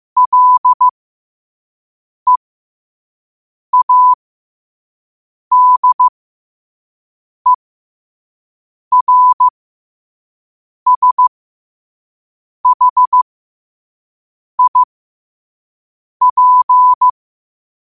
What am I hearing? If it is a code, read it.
Morse code, heard as a keyed tone, about 15 wpm, spaced as for 5 wpm, 1000 Hz. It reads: LEADERSHIP
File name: qotdfiles / 5wpm